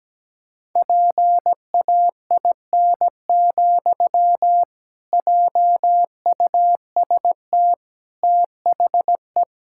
Morse code, heard as a keyed tone, about 17 wpm, 700 Hz. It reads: PAIN, JUST THE
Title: Sentences from Top 1600 Words 17wpm